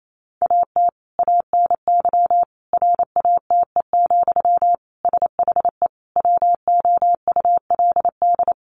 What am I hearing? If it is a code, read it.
Morse code, heard as a keyed tone, about 28 wpm, 700 Hz. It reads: AT ANY RATE, SHE WOULD
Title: Sentences from Top 1300 Words 28wpm